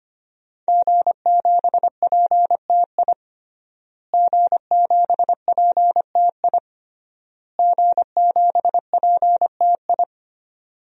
Morse code, heard as a keyed tone, about 25 wpm, 700 Hz. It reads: G7PTI G7PTI G7PTI
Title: DXCC-100 Call Signs - 5 Characters - Repeated 3x - 25wpm